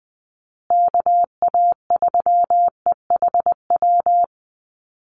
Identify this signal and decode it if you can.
Morse code, heard as a keyed tone, about 20 wpm, 700 Hz. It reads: KA3EHW
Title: USA Call Signs Repeated 3x 20wpm